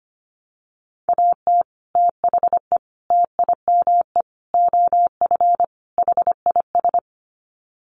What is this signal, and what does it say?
Morse code, heard as a keyed tone, about 25 wpm, 700 Hz. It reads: AT THE TIME OF HIS
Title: Sets of 5 Words Encore2 25wpm